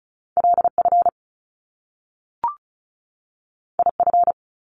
Morse code, heard as a keyed tone, about 36 wpm, 700 Hz. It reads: LF IF
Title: Top Two Letter Combinations in Words 35wpm